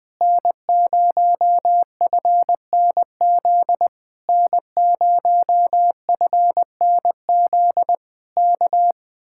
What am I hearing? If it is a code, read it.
Morse code, heard as a keyed tone, about 20 wpm, 700 Hz. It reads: N0FNZ N0FNZ K
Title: Speed-Racing Common QSO Phrases - 17wpm